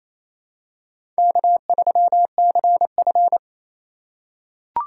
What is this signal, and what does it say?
Morse code, heard as a keyed tone, about 28 wpm, 700 Hz. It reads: K3CF
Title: USA Call Signs - 4 Characters - 3x Character Spacing - 28wpm